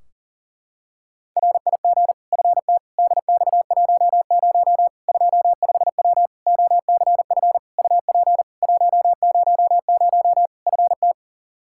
Morse code, heard as a keyed tone, about 40 wpm, 700 Hz. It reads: RIG FT DX10 25W OCF UP 100 FT
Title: Practice QSO - Exchange 2 - Info - 40wpm